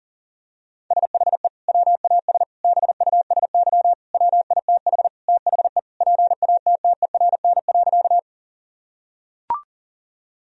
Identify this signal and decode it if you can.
Morse code, heard as a keyed tone, about 37 wpm, 700 Hz. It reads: SHE WAS BUSY WITH THE PATTERN.
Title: Sentences from Top 1000 Words - Review Part 2 - 40wpm